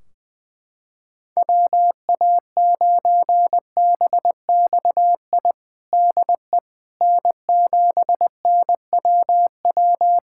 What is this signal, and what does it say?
Morse code, heard as a keyed tone, about 20 wpm, 700 Hz. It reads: WA9BXI DE N7NWW
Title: Practice QSO - Exchange 4 - Ending - 20wpm